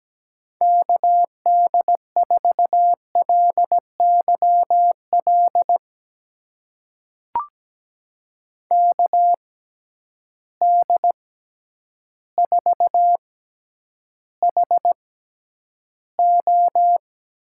Morse code, heard as a keyed tone, about 17 wpm, spaced as for 9 wpm, 700 Hz. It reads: KD4LYL KD4HO
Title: USA Call Signs - 6 Characters - 6x Character Spacing - 17wpm